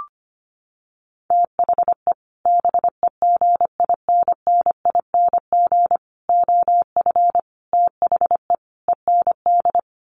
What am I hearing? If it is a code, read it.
Morse code, heard as a keyed tone, about 25 wpm, 700 Hz. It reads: THE BEGINNING OF THE END
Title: Sets of 5 Words Encore2 25wpm